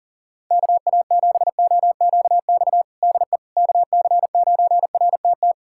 Morse code, heard as a keyed tone, about 37 wpm, 700 Hz. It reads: KA7OQX DE KC9RTT
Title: USA Callsign DE Callsign - 6 Characters - 40wpm